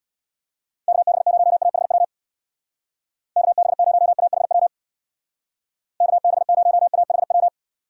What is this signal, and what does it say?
Morse code, heard as a keyed tone, about 59 wpm, 700 Hz. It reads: KB0AHW KB0AHW KB0AHW
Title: Speed Racing - USA Call Signs 50wpm